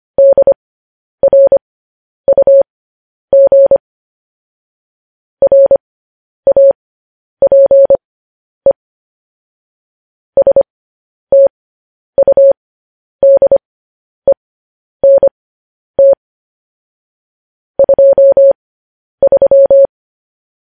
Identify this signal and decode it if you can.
Morse code, heard as a keyed tone, about 25 wpm, spaced as for 10 wpm, 550 Hz. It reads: DRUG RAPE STUDENT 23
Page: news headlines in morse code at 10 wpm